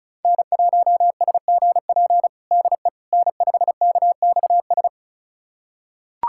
Morse code, heard as a keyed tone, about 36 wpm, 700 Hz. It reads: N1SGP DE N5KXS
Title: USA Callsign DE Callsign - 5 Characters - 35wpm